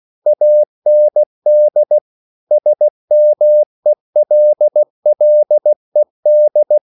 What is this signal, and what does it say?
Morse code, heard as a keyed tone, about 16 wpm, 600 Hz. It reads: AND SMELLED